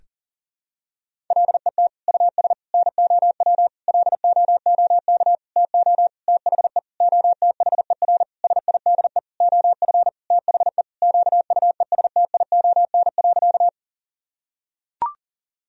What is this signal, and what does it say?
Morse code, heard as a keyed tone, about 40 wpm, 700 Hz. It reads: LET US NOW LOOK TO THE OTHER SIDE OF THE QUESTION.
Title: Sentences from Top 1000 Words - Review Part 1 - 40wpm